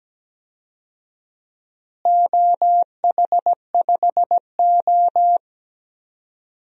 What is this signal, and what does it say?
Morse code, heard as a keyed tone, about 17 wpm, 700 Hz. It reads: OH5O
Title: DXCC-100 Call Signs - 4 Characters - Repeated 2x - 17wpm